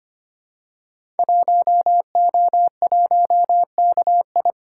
Morse code, heard as a keyed tone, about 25 wpm, 700 Hz. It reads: 1O1KI
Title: Code Groups - 5 Letter:Numbers - 25wpm